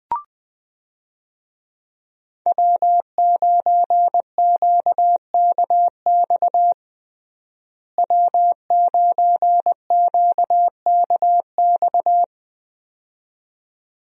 Morse code, heard as a keyed tone, about 20 wpm, 700 Hz. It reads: W9QKX W9QKX
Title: USA Call Signs - 5 Characters - Repeated 2x - 20wpm